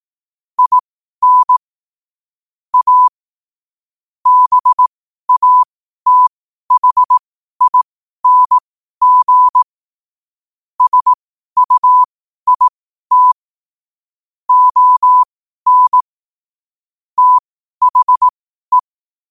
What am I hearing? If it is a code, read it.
Morse code, heard as a keyed tone, about 18 wpm, spaced as for 12 wpm, 1000 Hz. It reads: IN A BATHING SUIT ON THE
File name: qotdfiles / 12wpm